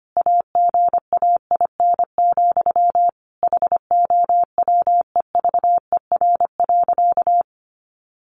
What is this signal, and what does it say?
Morse code, heard as a keyed tone, about 25 wpm, 700 Hz. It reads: AGAIN, HOWEVER.
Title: Sentences from Top 600 Words 25wpm